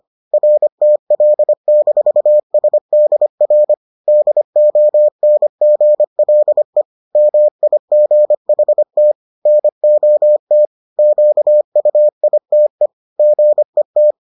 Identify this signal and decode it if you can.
Morse code, heard as a keyed tone, about 25 wpm, 600 Hz. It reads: RTL-SDR DONGLE MIGHT NOT QUITE GET